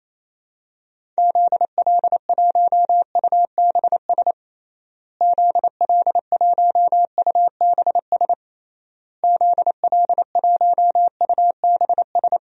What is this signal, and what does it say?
Morse code, heard as a keyed tone, about 28 wpm, 700 Hz. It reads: ZL1UBS ZL1UBS ZL1UBS
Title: DXCC-100 Call Signs - 6 Characters - Repeated 3x - 28wpm